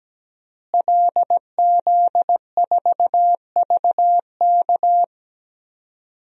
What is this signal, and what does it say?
Morse code, heard as a keyed tone, about 17 wpm, 700 Hz. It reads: LZ4VK
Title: DXCC-100 Call Signs - 5 Characters - Repeated 3x - 17wpm